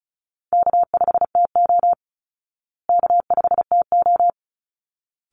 Morse code, heard as a keyed tone, about 34 wpm, 700 Hz. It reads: K5TO K5TO
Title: Call Signs Repeated 3x - 35wpm